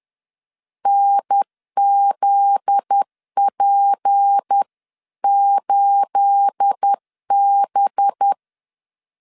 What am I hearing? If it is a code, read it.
Morse code, heard as a keyed tone, about 10 wpm, 800 Hz. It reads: NZP8B
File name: CW / DET / 1 9 1500C S A 10wpm 4511